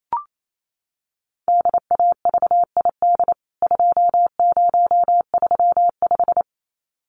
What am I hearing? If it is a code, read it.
Morse code, heard as a keyed tone, about 28 wpm, 700 Hz. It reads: DAVID 2035
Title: CWT Exchange 28wpm